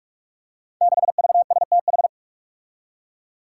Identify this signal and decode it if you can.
Morse code, heard as a keyed tone, about 44 wpm, 700 Hz. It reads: 6VSTH
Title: Five Challenging Letter-Number Combinations 45wpm